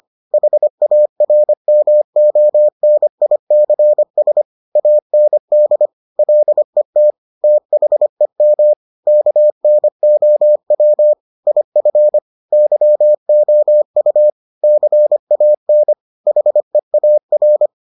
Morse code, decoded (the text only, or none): HARMONICS AND LET THEM KNOW IF YOU CAN HEAR